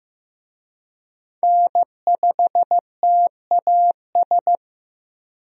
Morse code, heard as a keyed tone, about 15 wpm, 700 Hz. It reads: N5TAS
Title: USA Call Signs - 5 Characters - Repeated 3x - 15wpm